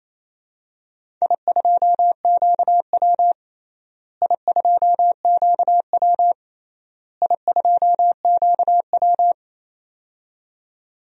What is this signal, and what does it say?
Morse code, heard as a keyed tone, about 28 wpm, 700 Hz. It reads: I2QW I2QW I2QW
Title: DXCC-100 Call Signs - 4 Characters - Repeated 3x - 28wpm